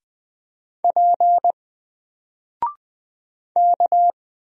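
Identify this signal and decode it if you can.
Morse code, heard as a keyed tone, about 20 wpm, 700 Hz. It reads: P K
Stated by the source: Single letter and number Morse drill at 20 wpm, from N4LCD.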